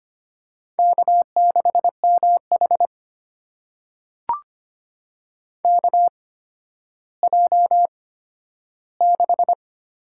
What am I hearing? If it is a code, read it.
Morse code, heard as a keyed tone, about 25 wpm, spaced as for 12 wpm, 700 Hz. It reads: K6MH KJ6
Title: USA Call Signs - 8x Character Spacing - 25wpm